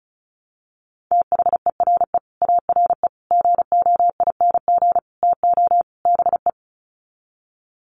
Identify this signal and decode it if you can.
Morse code, heard as a keyed tone, about 34 wpm, 700 Hz. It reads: THERE ARE GOING TO BE
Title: Sets of 5 Words 35wpm